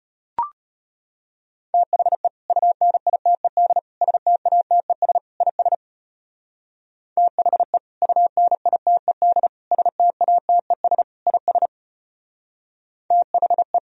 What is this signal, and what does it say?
Morse code, heard as a keyed tone, about 34 wpm, 700 Hz. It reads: THE UNITED STATES IS THE UNITED STATES IS THE